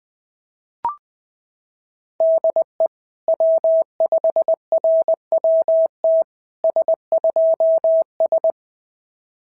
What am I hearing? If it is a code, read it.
Morse code, heard as a keyed tone, about 20 wpm, 650 Hz. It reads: DE W5RWT S2S
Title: SOTA QSO Practice 20wpm